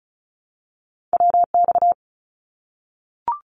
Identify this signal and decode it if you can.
Morse code, heard as a keyed tone, about 34 wpm, 700 Hz. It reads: WX
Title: Abbreviations Repeated 3x - 35wpm